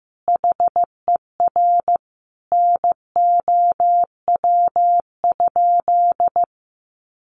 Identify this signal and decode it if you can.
Morse code, heard as a keyed tone, about 15 wpm, 700 Hz. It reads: HER NOW?
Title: Sentences from Top 900 Words 15wpm